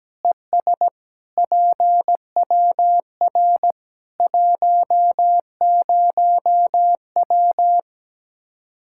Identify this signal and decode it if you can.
Morse code, heard as a keyed tone, about 17 wpm, 700 Hz. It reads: ES PWR 10W
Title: Practice QSO - Exchange 2 - Info - 17wpm